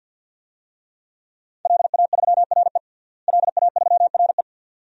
Morse code, heard as a keyed tone, about 50 wpm, 700 Hz. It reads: LA3RE LA3RE
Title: DXCC-100 Call Signs - 5 Characters - Repeated 2x - 50wpm